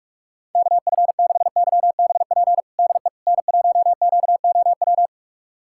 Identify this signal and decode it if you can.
Morse code, heard as a keyed tone, about 42 wpm, 700 Hz. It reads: KF6YBP DE N1QOW
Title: USA Callsign DE Callsign 45wpm